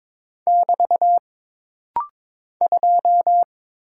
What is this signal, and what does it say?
Morse code, heard as a keyed tone, about 22 wpm, 700 Hz. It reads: <BT> 2
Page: This rapid-fire timing is geared to push your ICR (Instant Character Recognition) skills.